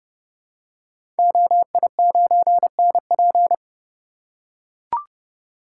Morse code, heard as a keyed tone, about 31 wpm, 700 Hz. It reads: OI9NP
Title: Code Groups - 5 Letter:Numbers - 30wpm